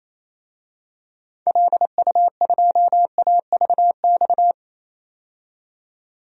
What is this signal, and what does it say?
Morse code, heard as a keyed tone, about 28 wpm, 700 Hz. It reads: LU2AVX
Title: DXCC-100 Call Signs - 6 Characters - 28wpm